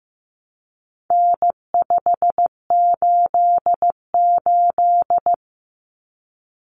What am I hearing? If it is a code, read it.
Morse code, heard as a keyed tone, about 15 wpm, 700 Hz. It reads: N588
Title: Common Cut Numbers - 4-Digit 15wpm